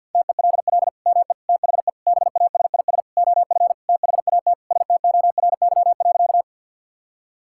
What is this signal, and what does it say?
Morse code, heard as a keyed tone, about 50 wpm, 700 Hz. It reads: TELL ME THE BASIS OF THAT STORY.
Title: Sentences from Top 1000 Words - Review Part 5 - 50wpm